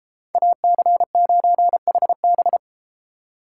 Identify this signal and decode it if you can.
Morse code, heard as a keyed tone, about 32 wpm, 700 Hz. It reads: AC9HB